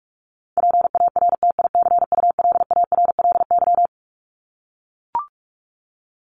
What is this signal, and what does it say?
Morse code, heard as a keyed tone, about 44 wpm, 700 Hz. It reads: PARTICULARLY
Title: LY Suffix - Top 100 Words with Base Words 45wpm